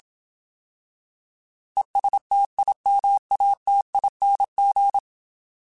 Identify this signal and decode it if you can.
Morse code, heard as a keyed tone, about 27 wpm, 800 Hz. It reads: ESTIMATING
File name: Alphabetical / Alphabetical mp3s / Es @25